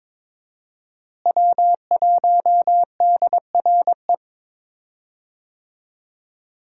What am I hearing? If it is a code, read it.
Morse code, heard as a keyed tone, about 22 wpm, 700 Hz. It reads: W1DRE
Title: USA Call Signs - 5 Characters - Repeated 2x - 22wpm